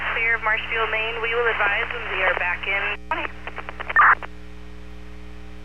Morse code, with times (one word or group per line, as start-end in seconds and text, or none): none